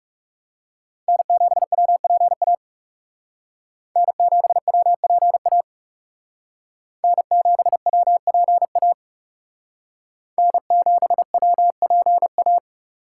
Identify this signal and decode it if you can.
Morse code, heard as a keyed tone, about 36 wpm, 700 Hz. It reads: N7WPA N7WPA N7WPA N7WPA
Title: Speed Racing - USA Call Signs 30wpm